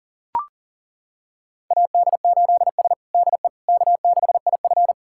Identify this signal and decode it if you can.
Morse code, heard as a keyed tone, about 40 wpm, 700 Hz. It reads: AD8S DE K6IF
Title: USA Callsign DE Callsign - 4 Characters - 40wpm